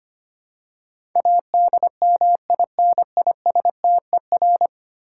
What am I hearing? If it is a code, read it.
Morse code, heard as a keyed tone, about 25 wpm, 690 Hz. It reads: ADMINISTER
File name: 10 Letter Words - First 3 Letters Given - 25wpm